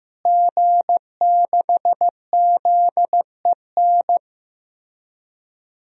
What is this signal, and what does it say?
Morse code, heard as a keyed tone, about 16 wpm, 700 Hz. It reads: G6ZEN
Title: DXCC-100 Call Signs - 5 Characters - Repeated 3x - 15wpm